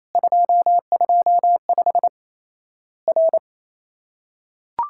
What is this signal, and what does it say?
Morse code, heard as a keyed tone, about 27 wpm, 700 Hz. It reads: 225 R
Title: Worked All Europe DX Contest - QTC Exchange - 28wpm